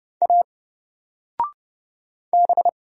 Morse code, heard as a keyed tone, about 30 wpm, 700 Hz. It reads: A B
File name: Single-Letter-Number-30wpm